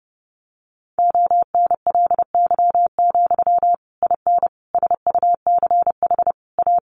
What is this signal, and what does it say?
Morse code, heard as a keyed tone, about 30 wpm, 700 Hz. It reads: ONLY, IN SUCH A
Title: Sentences from Top 1500 Words 30wpm